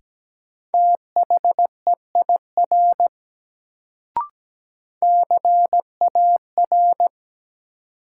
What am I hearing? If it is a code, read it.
Morse code, heard as a keyed tone, about 17 wpm, 700 Hz. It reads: THEIR CAR
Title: Sentences from Top 300 Words - New Word Drill - 17wpm